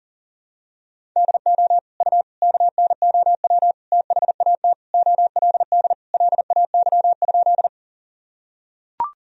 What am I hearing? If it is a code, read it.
Morse code, heard as a keyed tone, about 40 wpm, 700 Hz. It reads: DO U KNOW THAT OLD LAY?
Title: Sentences from Top 400 Words and Abbreviations 40wpm